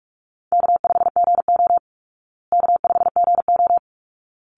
Morse code, heard as a keyed tone, about 44 wpm, 700 Hz. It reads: K5GO K5GO
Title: Call Signs Repeated 3x - 45wpm